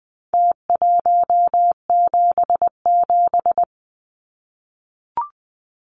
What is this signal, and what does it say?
Morse code, heard as a keyed tone, about 20 wpm, 700 Hz. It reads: T177
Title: Numbers - Contest Style - 4-Digit 20wpm